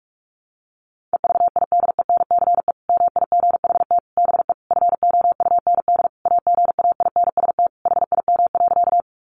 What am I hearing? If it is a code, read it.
Morse code, heard as a keyed tone, about 44 wpm, 700 Hz. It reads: EVIDENCE MIGHT BE FOUND AGAINST HIM.
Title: Sentences from Top 1300 Words 45wpm